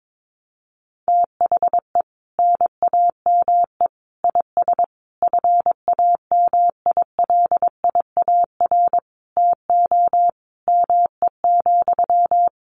Morse code, heard as a keyed tone, about 22 wpm, 700 Hz. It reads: THE NAME IS FAMILIAR TO ME,
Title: Sentences from Top 1000 Words 22wpm